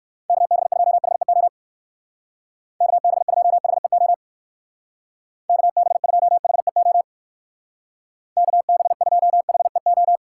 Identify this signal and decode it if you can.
Morse code, heard as a keyed tone, about 57 wpm, 700 Hz. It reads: KB2HEO KB2HEO KB2HEO KB2HEO